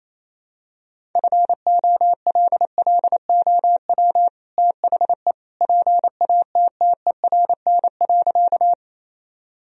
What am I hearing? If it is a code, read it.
Morse code, heard as a keyed tone, about 27 wpm, 700 Hz. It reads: FOLLOW THE PATTERN.
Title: Sentences from Top 1000 Words - Review Part 2 - 28wpm